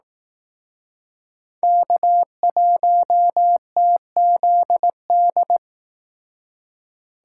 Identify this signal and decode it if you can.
Morse code, heard as a keyed tone, about 18 wpm, 700 Hz. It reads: K1TZD
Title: Speed Racing - USA Call Signs 15wpm